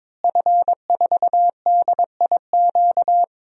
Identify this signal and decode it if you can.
Morse code, heard as a keyed tone, about 21 wpm, 700 Hz. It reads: F4DIQ
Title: DXCC-100 Call Signs - 5 Characters - Repeated 3x - 22wpm